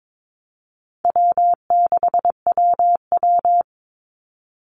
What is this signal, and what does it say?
Morse code, heard as a keyed tone, about 22 wpm, 700 Hz. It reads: W6WW
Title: NA SOTA Chaser Callsigns 22wpm